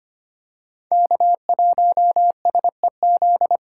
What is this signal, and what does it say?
Morse code, heard as a keyed tone, about 25 wpm, 700 Hz. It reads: K1SEZ
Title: USA Call Signs - 5 Characters - 3x Character Spacing - 25wpm